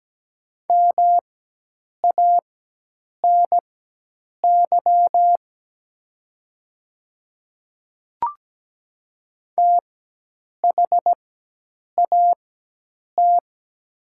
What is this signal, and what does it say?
Morse code, heard as a keyed tone, about 17 wpm, spaced as for 8 wpm, 700 Hz. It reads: MANY THAT
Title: 4x Character Spacing - Top 300 Words - 17wpm